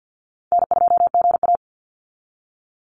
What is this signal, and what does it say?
Morse code, heard as a keyed tone, about 49 wpm, 700 Hz. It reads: N2GA
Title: Easy Call Signs Repeated 3x - 50wpm